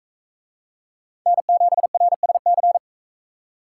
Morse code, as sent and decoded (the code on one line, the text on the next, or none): -. --... .-. ... -.-.
N7RSC